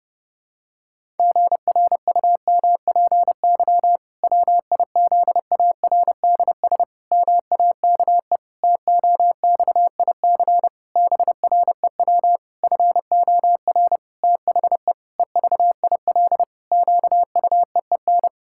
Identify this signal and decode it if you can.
Morse code, heard as a keyed tone, about 30 wpm, 700 Hz. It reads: GRUMPY WIZARDS MAKE TOXIC BREW FOR THE EVIL QUEEN
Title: Pangram Challenge v2 - 30wpm